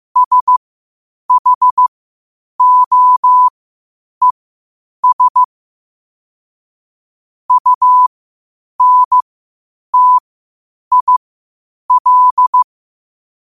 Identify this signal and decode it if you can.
Morse code, heard as a keyed tone, about 15 wpm, spaced as for 8 wpm, 1000 Hz. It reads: SHOES UNTIL
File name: qotdfiles / 8wpm